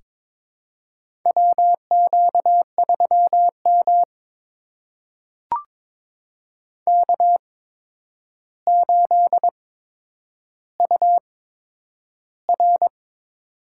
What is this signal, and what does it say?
Morse code, heard as a keyed tone, about 22 wpm, spaced as for 9 wpm, 700 Hz. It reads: WQ3M K8UR
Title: USA Call Signs - 4 Characters - 8x Character Spacing - 22wpm